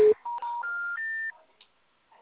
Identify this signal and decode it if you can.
Telephone call progress tone, heard as a telephone network's tone. Special information tone.